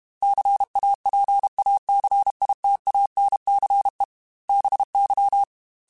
Morse code, heard as a keyed tone, about 32 wpm, 800 Hz. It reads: CAPACITANCE BY